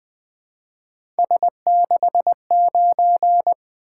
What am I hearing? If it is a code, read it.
Morse code, heard as a keyed tone, about 20 wpm, 700 Hz. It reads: S69
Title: Three Letter-Number Combinations - 20wpm